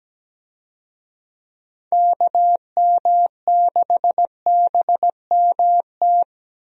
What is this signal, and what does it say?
Morse code, heard as a keyed tone, about 17 wpm, 700 Hz. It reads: KM6BMT
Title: USA Call Signs - 6 Characters - Repeated 3x - 17wpm